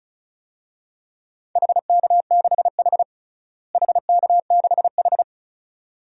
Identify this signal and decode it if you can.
Morse code, heard as a keyed tone, about 34 wpm, 700 Hz. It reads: HK6H HK6H
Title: DXCC-100 Call Signs - 4 Characters - Repeated 2x - 35wpm